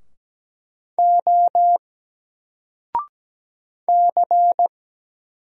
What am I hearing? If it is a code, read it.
Morse code, heard as a keyed tone, about 17 wpm, 700 Hz. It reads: O C
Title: Single Letter-Number 17wpm